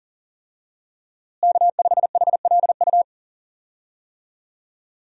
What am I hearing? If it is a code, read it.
Morse code, heard as a keyed tone, about 40 wpm, 700 Hz. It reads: K5HLU
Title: USA Call Signs - 5 Characters - Repeated 2x - 40wpm